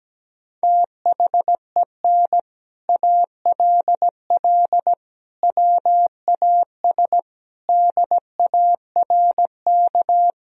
Morse code, heard as a keyed tone, about 17 wpm, 700 Hz. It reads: THEN ALL WAS DARK
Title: Sentences from Top 500 Words and Abbreviations 17wpm